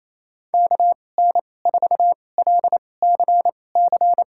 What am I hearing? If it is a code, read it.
Morse code, heard as a keyed tone, about 28 wpm, spaced as for 20 wpm, 700 Hz. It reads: KN4LCC
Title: USA Call Signs - 6 Characters - 2x Character Spacing - 28wpm